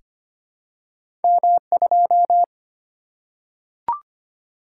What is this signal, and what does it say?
Morse code, heard as a keyed tone, about 25 wpm, 700 Hz. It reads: M2
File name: Two Letter-Number Combinations - 25wpm